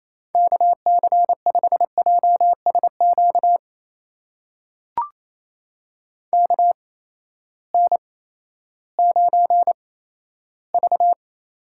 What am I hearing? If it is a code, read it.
Morse code, heard as a keyed tone, about 28 wpm, spaced as for 14 wpm, 700 Hz. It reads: KC5JSQ KN9V
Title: USA Call Signs - 8x Character Spacing - 28wpm